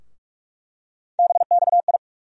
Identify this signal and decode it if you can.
Morse code, heard as a keyed tone, about 46 wpm, 700 Hz. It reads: BXI
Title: USA Call Sign Postfixes - 3 Letters 45wpm